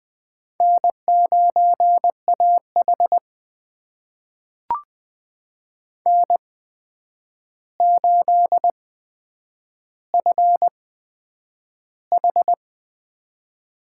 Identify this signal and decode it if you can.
Morse code, heard as a keyed tone, about 20 wpm, spaced as for 9 wpm, 700 Hz. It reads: N9AH N8FH
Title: USA Call Signs - 8x Character Spacing - 20wpm